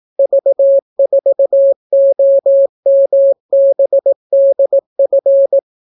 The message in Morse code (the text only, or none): V4OMBDF